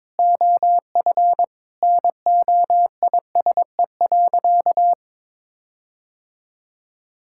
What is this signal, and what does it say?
Morse code, heard as a keyed tone, about 22 wpm, 700 Hz. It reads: OF NOISE.
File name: practice / 22wpm / Sentences with 4-Word Phrases - 22wpm